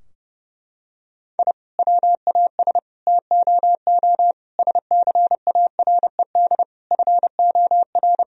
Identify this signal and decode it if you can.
Morse code, heard as a keyed tone, about 30 wpm, 700 Hz. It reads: I WAS TOO SCARED FOR
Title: Sentences from Top 1000 Words - Review Part 4 - 30wpm